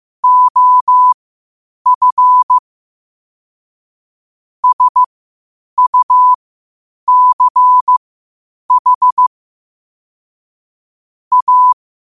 Morse code, heard as a keyed tone, about 15 wpm, spaced as for 8 wpm, 1000 Hz. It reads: OF SUCH A
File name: qotdfiles / 8wpm